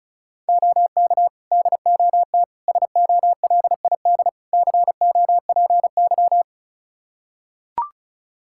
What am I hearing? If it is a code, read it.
Morse code, heard as a keyed tone, about 36 wpm, 700 Hz. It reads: OK DOT SOLID COPY
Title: Practice QSO - Exchange 3 - More Info - 35wpm